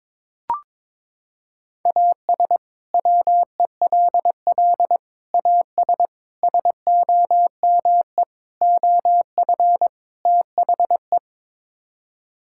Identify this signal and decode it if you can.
Morse code, heard as a keyed tone, about 22 wpm, 700 Hz. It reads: AS WELL AS SOME OF THE
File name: practice / 22wpm / Sets of 6 Words - 22wpm